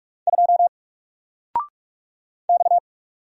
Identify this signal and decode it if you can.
Morse code, heard as a keyed tone, about 44 wpm, 700 Hz. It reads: 2 X
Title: Single Character (Letter:Number:Punctuation:Prosign) Alt 45wpm